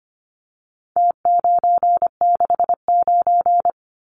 Morse code, heard as a keyed tone, about 25 wpm, 700 Hz. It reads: T969
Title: Numbers - Contest Style - 4-Digit 25wpm